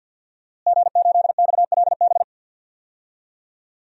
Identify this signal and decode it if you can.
Morse code, heard as a keyed tone, about 48 wpm, 700 Hz. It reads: G8XLB
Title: DXCC-100 Call Signs - 5 Characters - Repeated 3x - 50wpm